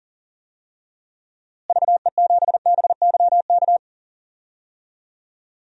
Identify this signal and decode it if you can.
Morse code, heard as a keyed tone, about 40 wpm, 700 Hz. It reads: VE7BYK
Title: DXCC-100 Call Signs - 6 Characters - Repeated 2x - 40wpm